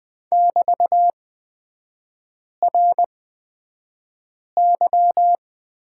Morse code, heard as a keyed tone, about 20 wpm, spaced as for 5 wpm, 700 Hz. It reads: <BT>RY
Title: 224 Lesson 36e - BT - Three Characters 20wpm